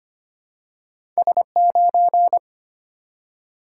Morse code, heard as a keyed tone, about 25 wpm, 700 Hz. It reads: S9
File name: Two Letter-Number Combinations - 25wpm